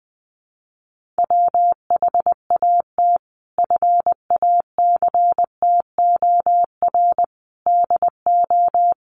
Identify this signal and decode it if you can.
Morse code, heard as a keyed tone, about 20 wpm, 700 Hz. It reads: WHAT FACTOR DO
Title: Sentences from Top 1300 Words 20wpm